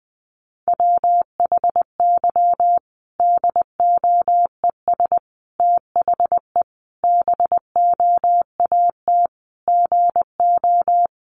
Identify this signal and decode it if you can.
Morse code, heard as a keyed tone, about 19 wpm, 700 Hz. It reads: WHY DOES THE BOAT GO
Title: Sentences from Top 500 Words 20wpm